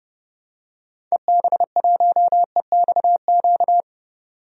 Morse code, heard as a keyed tone, about 30 wpm, 700 Hz. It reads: EB1EXQ
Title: DXCC-100 Call Signs - 6 Characters - 30wpm